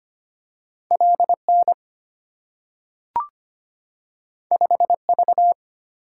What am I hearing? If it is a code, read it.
Morse code, heard as a keyed tone, about 25 wpm, 700 Hz. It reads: LN 5V
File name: Two Letter-Number Combinations - 25wpm